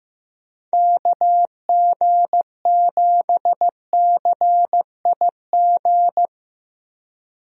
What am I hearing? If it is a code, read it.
Morse code, heard as a keyed tone, about 15 wpm, 700 Hz. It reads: KG7CIG
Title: USA Call Signs - 6 Characters - Repeated 2x - 15wpm